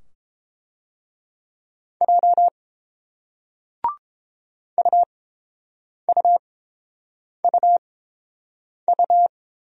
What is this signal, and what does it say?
Morse code, heard as a keyed tone, about 30 wpm, spaced as for 7 wpm, 700 Hz. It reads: J UUUU